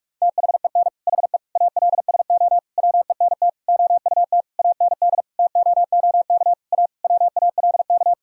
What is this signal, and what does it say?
Morse code, heard as a keyed tone, about 41 wpm, 700 Hz. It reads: THEN HE ALSO WENT OUT AND TOOK A WALK
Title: Sentences from Top 1600 Words 45wpm